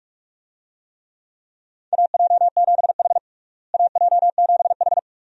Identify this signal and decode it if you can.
Morse code, heard as a keyed tone, about 48 wpm, 700 Hz. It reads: AJ7H AJ7H
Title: USA Call Signs - 4 Characters - Repeated 2x - 45wpm